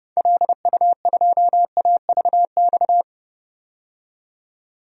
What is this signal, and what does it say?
Morse code, heard as a keyed tone, about 32 wpm, 700 Hz. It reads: LU2AVX
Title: DXCC-100 Call Signs - 6 Characters - 30wpm